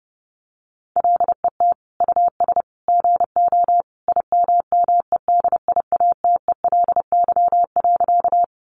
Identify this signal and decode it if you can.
Morse code, heard as a keyed tone, about 30 wpm, 700 Hz. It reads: LET US GO IMMEDIATELY.
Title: Sentences from Top 1100 Words 30wpm